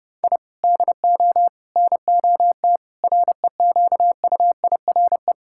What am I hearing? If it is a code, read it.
Morse code, heard as a keyed tone, about 32 wpm, 700 Hz. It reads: I DO NOT REQUIRE
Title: Sentences from Top 1000 Words - Review Part 6 - 30wpm